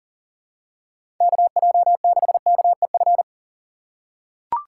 Morse code, heard as a keyed tone, about 40 wpm, 700 Hz. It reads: KJ6KEF
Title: USA Call Signs - 6 Characters - 2x Character Spacing - 40wpm